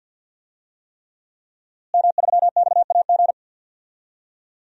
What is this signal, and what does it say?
Morse code, heard as a keyed tone, about 47 wpm, 700 Hz. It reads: M3XAG
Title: DXCC-100 Call Signs - 5 Characters - Repeated 3x - 50wpm